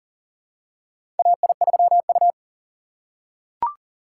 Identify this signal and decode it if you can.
Morse code, heard as a keyed tone, about 40 wpm, 690 Hz. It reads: AI3U
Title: USA Call Signs - 4 Characters - 8x Character Spacing - 40wpm